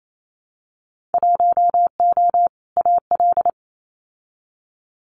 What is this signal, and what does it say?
Morse code, heard as a keyed tone, about 28 wpm, 700 Hz. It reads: 1O AL
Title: Winter Field Day Exchanges 28wpm